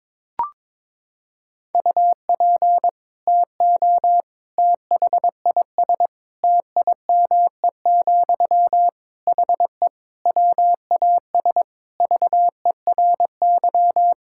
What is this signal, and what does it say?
Morse code, heard as a keyed tone, about 22 wpm, 700 Hz. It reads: UP TO THIS TIME, HE WAS VERY
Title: Sentences from Top 1000 Words - Review Part 6 - 22wpm